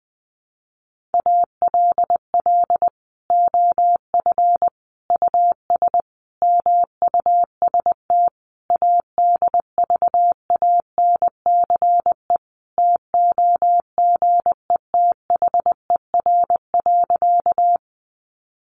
Morse code, heard as a keyed tone, about 20 wpm, 700 Hz. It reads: ALL OF US MUST ADVANCE TOGETHER.
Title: Sentences from Top 1400 Words 20wpm